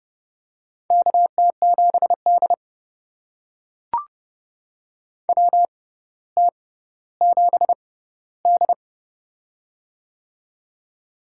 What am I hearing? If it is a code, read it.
Morse code, heard as a keyed tone, about 30 wpm, spaced as for 15 wpm, 700 Hz. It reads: KT7D WT7D
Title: USA Call Signs - 4 Characters - 6x Character Spacing - 30wpm